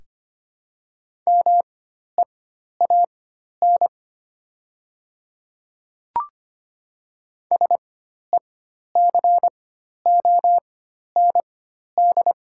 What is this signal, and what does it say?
Morse code, heard as a keyed tone, about 25 wpm, spaced as for 12 wpm, 700 Hz. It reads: MEAN SECOND
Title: 4x Character Spacing - Top 500 Words - 25wpm